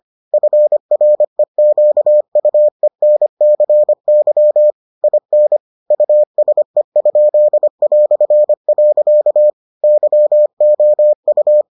Morse code, heard as a keyed tone, about 25 wpm, 600 Hz. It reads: FREQUENCY IN USE?". YOU